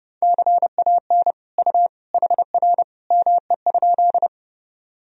Morse code, heard as a keyed tone, about 30 wpm, 700 Hz. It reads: CAN U HR ME?
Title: Sentences from Top 300 Words and Abbreviations 30wpm